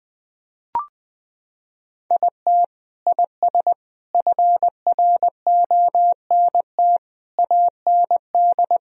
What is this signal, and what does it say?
Morse code, heard as a keyed tone, about 20 wpm, 700 Hz. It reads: IT IS FRONT AND
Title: Sentences from Top 1000 Words - Review Part 2 - 20wpm